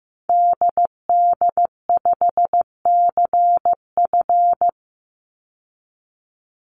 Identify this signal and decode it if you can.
Morse code, heard as a keyed tone, about 15 wpm, 700 Hz. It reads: DD5CF
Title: Call Signs Repeated 3x - 15wpm